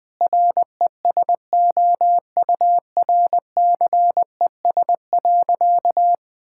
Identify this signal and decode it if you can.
Morse code, heard as a keyed tone, about 20 wpm, 700 Hz. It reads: RESOURCES.
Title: Sentences with 2-Word Phrases - Spoken Beginning - 20wpm